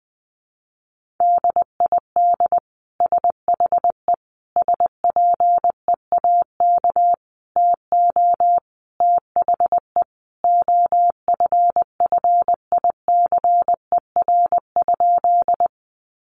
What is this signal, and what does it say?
Morse code, heard as a keyed tone, about 20 wpm, 700 Hz. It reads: DID SHE SPEAK TO THE OFFICER?
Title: Sentences from Top 1200 Words 20wpm